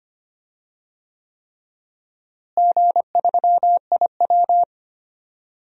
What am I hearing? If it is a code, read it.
Morse code, heard as a keyed tone, about 25 wpm, 700 Hz. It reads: G3IW
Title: DXCC-100 Call Signs - 4 Characters - Repeated 3x - 25wpm